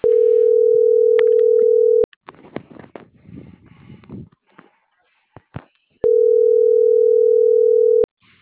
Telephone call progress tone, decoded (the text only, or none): ringback tone